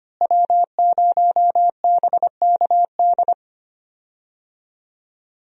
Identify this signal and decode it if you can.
Morse code, heard as a keyed tone, about 25 wpm, 700 Hz. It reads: W0BKD
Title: USA Call Signs - 5 Characters - Repeated 2x - 25wpm